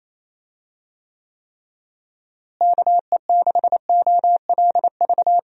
Morse code, heard as a keyed tone, about 27 wpm, 700 Hz. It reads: KE6OLV
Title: USA Call Signs - 6 Characters - Repeated 3x - 28wpm